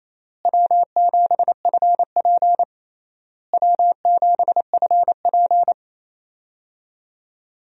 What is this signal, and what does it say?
Morse code, heard as a keyed tone, about 28 wpm, 700 Hz. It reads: W7FP W7FP
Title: USA Call Signs - 4 Characters - Repeated 3x - 28wpm